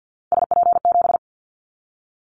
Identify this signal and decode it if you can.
Morse code, heard as a keyed tone, about 50 wpm, 700 Hz. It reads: SP7
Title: Callsign Prefixes for Top 25 DXCC - 50wpm